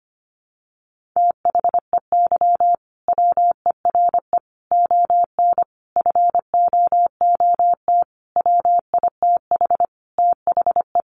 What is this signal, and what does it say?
Morse code, heard as a keyed tone, about 25 wpm, 700 Hz. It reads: THEY WERE ON FOOT WITH THE
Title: Sentences from Top 1500 Words 25wpm